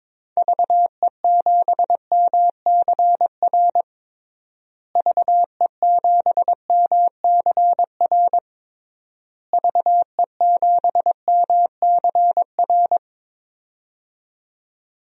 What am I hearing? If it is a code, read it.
Morse code, heard as a keyed tone, about 22 wpm, 700 Hz. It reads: VE7MCR VE7MCR VE7MCR
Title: DXCC-100 Call Signs - 6 Characters - Repeated 3x - 22wpm